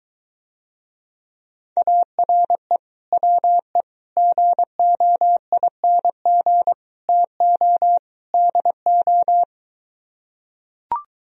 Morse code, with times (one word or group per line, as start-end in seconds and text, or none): 1.8-2.8 ARE
3.1-3.8 WE
4.2-6.7 GOING
7.1-8.0 TO
8.3-9.4 DO